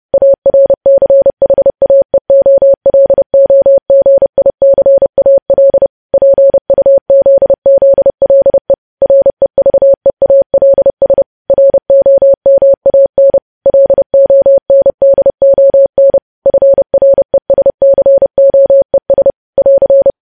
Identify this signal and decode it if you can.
Morse code, heard as a keyed tone, about 30 wpm, 550 Hz. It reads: ARCHAEOLOGICAL PUZZLE REVEALS ROMAN LONDON FRESCOES <AR>
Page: news headlines in morse code at 30 wpm